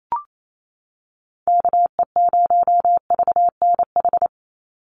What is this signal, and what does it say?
Morse code, heard as a keyed tone, about 28 wpm, 700 Hz. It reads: KE0VNH
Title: USA Call Signs 28wpm